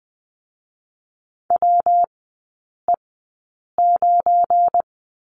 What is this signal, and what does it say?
Morse code, heard as a keyed tone, about 20 wpm, 700 Hz. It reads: W E 9
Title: 212 Lesson 34e - Number 8 - Three Characters 20wpm